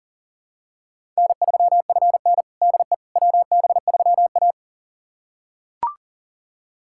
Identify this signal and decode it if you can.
Morse code, heard as a keyed tone, about 40 wpm, 700 Hz. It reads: N3FN DE WB3A
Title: USA Callsign DE Callsign - 4 Characters - 40wpm